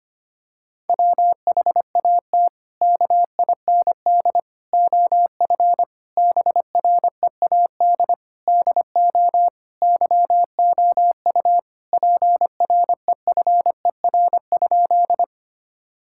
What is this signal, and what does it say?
Morse code, heard as a keyed tone, about 25 wpm, 700 Hz. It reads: WHAT KIND OF BREAD DO YOU PREFER?
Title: Sentences from Top 1600 Words 25wpm